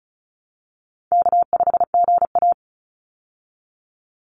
Morse code, heard as a keyed tone, about 34 wpm, 700 Hz. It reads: K5GA
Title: Easy Call Signs Repeated 3x - 35wpm